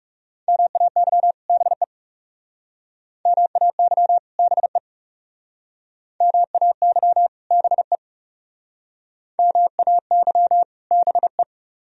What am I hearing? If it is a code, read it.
Morse code, heard as a keyed tone, about 38 wpm, 700 Hz. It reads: MAY BE MAY BE MAY BE MAY BE